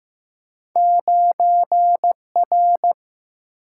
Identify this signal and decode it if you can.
Morse code, heard as a keyed tone, about 15 wpm, 700 Hz. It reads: 9R
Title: Single-Letter Single-Number 15wpm